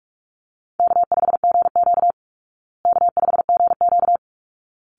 Morse code, heard as a keyed tone, about 44 wpm, 700 Hz. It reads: K5GQ K5GQ
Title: Call Signs Repeated 3x - 45wpm